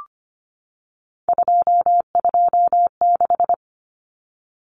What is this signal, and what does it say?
Morse code, heard as a keyed tone, about 25 wpm, 700 Hz. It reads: 226
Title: Numbers - 3-Digit 25wpm